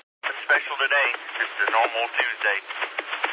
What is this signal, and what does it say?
Emergency Alert System (SAME) tones, heard as harsh buzzing bursts, with speech. It is heard from 0.2 to 2.7 s.